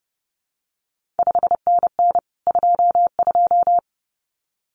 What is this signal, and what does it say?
Morse code, heard as a keyed tone, about 30 wpm, 700 Hz. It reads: 5NN 22
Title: CQ WW DX Contest Exchange Encore - 30wpm